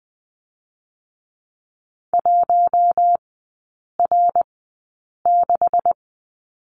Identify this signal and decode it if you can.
Morse code, heard as a keyed tone, about 20 wpm, 700 Hz. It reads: 1 R 6
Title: 127 Lesson 21e - Number 6 - Three Characters 20wpm